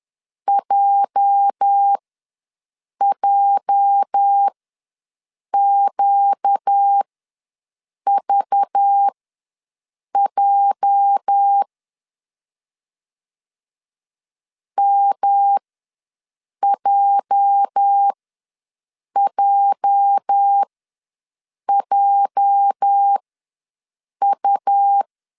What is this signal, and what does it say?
Morse code, heard as a keyed tone, about 11 wpm, spaced as for 6 wpm, 800 Hz. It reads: JJQVJ MJJJU
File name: LEZIONI / 8 YJ45 / 8 YJ ws 10wpm 4043